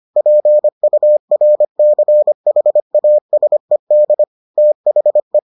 Morse code, heard as a keyed tone, about 25 wpm, 600 Hz. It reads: PURCHASED THE